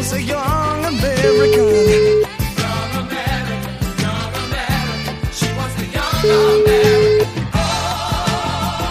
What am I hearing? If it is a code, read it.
Ringback tone (Europe, 425 Hz), with noise.